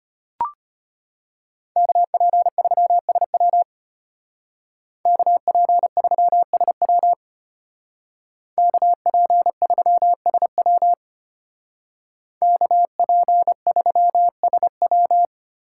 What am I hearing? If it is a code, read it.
Morse code, heard as a keyed tone, about 32 wpm, 700 Hz. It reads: KP3SW KP3SW KP3SW KP3SW